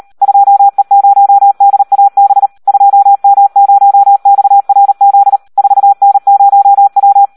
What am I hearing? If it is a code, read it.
Morse code, heard as a keyed tone, about 38 wpm, 800 Hz. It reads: 2E0DAB 2M0XRZ 4N0W
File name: Audio practica CW / SD 1 38